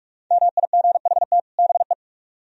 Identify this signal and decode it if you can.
Morse code, heard as a keyed tone, about 46 wpm, 700 Hz. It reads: MIGHT BE